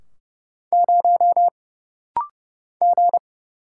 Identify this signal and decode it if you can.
Morse code, heard as a keyed tone, about 30 wpm, 700 Hz. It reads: 0 G
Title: Single Letter-Number - Rapid-Fire 30wpm